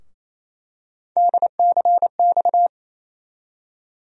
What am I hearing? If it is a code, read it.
Morse code, heard as a keyed tone, about 27 wpm, 700 Hz. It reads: DCX
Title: USA Call Sign Postfixes - 3 Letters 28wpm